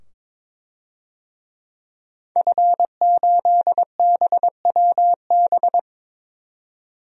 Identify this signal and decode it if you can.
Morse code, heard as a keyed tone, about 22 wpm, 700 Hz. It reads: F8BWB
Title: DXCC-100 Call Signs - 5 Characters - 22wpm